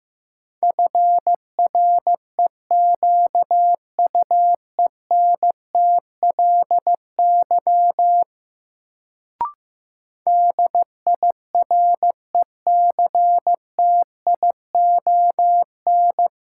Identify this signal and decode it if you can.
Morse code, heard as a keyed tone, about 15 wpm, 700 Hz. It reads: FREQUENTLY DIRECTION
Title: Sentences from Top 1300 Words - New Word Drill Alt - 15wpm